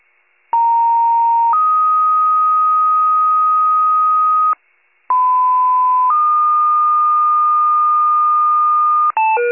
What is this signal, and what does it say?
Two-tone page, heard as a one-second tone, then a three-second tone, about 930 then 1250 Hz, about 1010 then 1220 Hz. It is heard from 0.5 to 9.1 s.